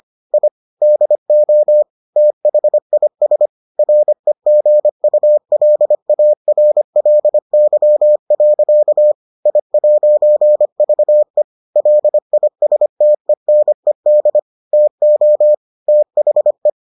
Morse code, heard as a keyed tone, about 25 wpm, 600 Hz. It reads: I DO THIS REGULARLY. I'VE LISTENED TO THE